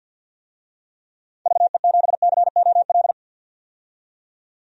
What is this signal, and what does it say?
Morse code, heard as a keyed tone, about 51 wpm, 700 Hz. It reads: VE7COL
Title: DXCC-100 Call Signs - 6 Characters - 50wpm